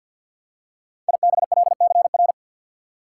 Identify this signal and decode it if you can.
Morse code, heard as a keyed tone, about 53 wpm, 700 Hz. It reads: IBLCR
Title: Code Groups - 5 Letter:Numbers - 50wpm